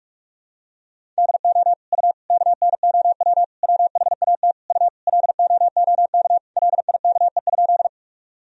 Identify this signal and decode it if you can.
Morse code, heard as a keyed tone, about 47 wpm, 700 Hz. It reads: DO U KNOW WHAT U LOOK LIKE?
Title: Sentences from Top 100 Words and Abbreviations 45wpm